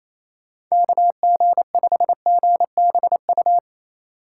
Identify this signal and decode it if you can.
Morse code, heard as a keyed tone, about 28 wpm, 700 Hz. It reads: KG5GBU
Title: USA Call Signs - 6 Characters - 3x Character Spacing - 28wpm